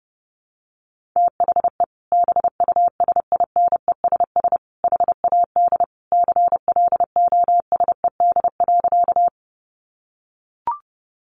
Morse code, heard as a keyed tone, about 30 wpm, 700 Hz. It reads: THE BUSINESS HAD CLOSED.
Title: Sentences from Top 1400 Words 30wpm